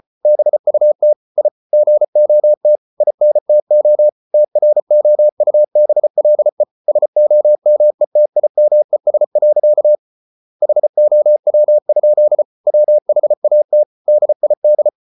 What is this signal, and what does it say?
Morse code, heard as a keyed tone, about 34 wpm, 600 Hz. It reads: BUT I GOT INTO TROUBLE SOMETIMES. HOW? WHAT DID